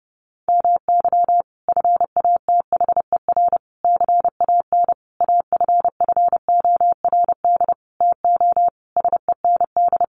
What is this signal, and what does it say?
Morse code, heard as a keyed tone, about 30 wpm, 700 Hz. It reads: MY FATHER CAN AFFORD TO SEND
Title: Sentences from Top 1500 Words 30wpm